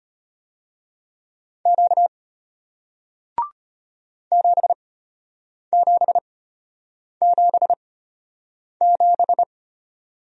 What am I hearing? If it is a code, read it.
Morse code, heard as a keyed tone, about 33 wpm, spaced as for 8 wpm, 700 Hz. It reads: Q 7777